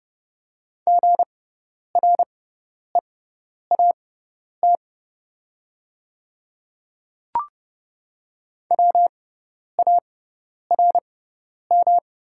Morse code, heard as a keyed tone, about 30 wpm, spaced as for 10 wpm, 700 Hz. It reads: GREAT WARM
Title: 6x Character Spacing - Top 500 Words - 30wpm